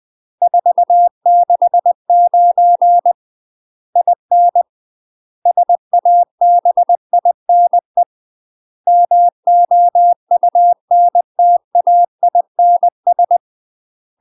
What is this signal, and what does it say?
Morse code, heard as a keyed tone, about 20 wpm, 700 Hz. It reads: 469 IN SABINE MOUNTAINS
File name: UK MASTER QSOs TRAINING / 20WPM / UK MASTER QSOs 0037